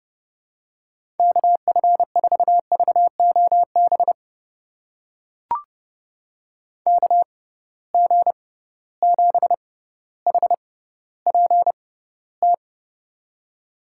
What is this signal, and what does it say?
Morse code, heard as a keyed tone, about 30 wpm, spaced as for 15 wpm, 700 Hz. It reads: KF4VOB KG7HPT
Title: USA Call Signs - 6 Characters - 6x Character Spacing - 30wpm